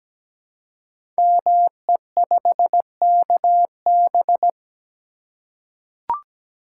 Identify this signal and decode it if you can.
Morse code, heard as a keyed tone, about 17 wpm, 700 Hz. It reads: ME5KB
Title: Code Groups - 5 Letter:Numbers - 17wpm